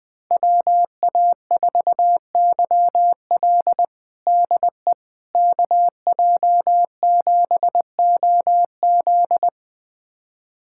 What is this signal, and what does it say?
Morse code, heard as a keyed tone, about 20 wpm, 700 Hz. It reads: WA4YL DE KJ7OZ
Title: USA Callsign DE Callsign 20wpm